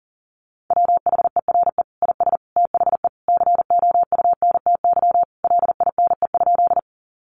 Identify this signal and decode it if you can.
Morse code, heard as a keyed tone, about 40 wpm, 700 Hz. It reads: WHERE IS THE COUNTY LINE?
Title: Sentences from Top 1200 Words 40wpm